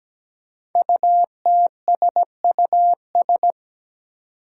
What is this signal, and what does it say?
Morse code, heard as a keyed tone, about 17 wpm, 700 Hz. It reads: UTSUS
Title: Five Challenging Letter-Number Combinations 17wpm